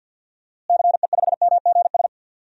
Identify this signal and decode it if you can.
Morse code, heard as a keyed tone, about 52 wpm, 700 Hz. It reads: CE5MGS
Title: DXCC-100 Call Signs - 6 Characters - Repeated 3x - 50wpm